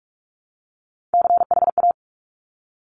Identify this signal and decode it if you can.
Morse code, heard as a keyed tone, about 46 wpm, 700 Hz. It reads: CHA
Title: Top Three Letter Combinations in Words 45wpm